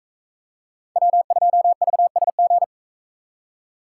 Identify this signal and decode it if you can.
Morse code, heard as a keyed tone, about 44 wpm, 700 Hz. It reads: W2VSG